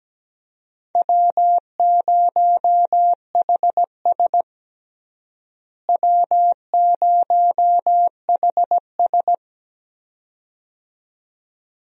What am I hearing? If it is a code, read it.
Morse code, heard as a keyed tone, about 17 wpm, 700 Hz. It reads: W0HS W0HS
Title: USA Call Signs - 4 Characters - Repeated 2x - 17wpm